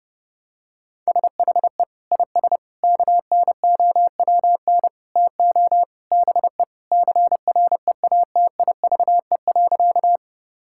Morse code, heard as a keyed tone, about 30 wpm, 700 Hz. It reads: SHE IS KNOWN TO BE CREATIVE.
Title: Sentences from Top 1000 Words - Review Part 5 - 30wpm